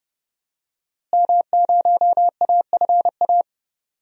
Morse code, heard as a keyed tone, about 30 wpm, 700 Hz. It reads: M0AFA
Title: DXCC-100 Call Signs - 5 Characters - Repeated 2x - 30wpm